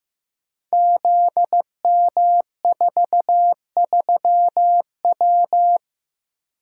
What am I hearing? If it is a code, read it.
Morse code, heard as a keyed tone, about 16 wpm, 700 Hz. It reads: ZM43W
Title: Code Groups - 5 Letter:Numbers - 15wpm